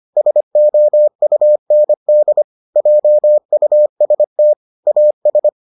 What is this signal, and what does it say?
Morse code, heard as a keyed tone, about 25 wpm, 600 Hz. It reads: SOUND JUST AS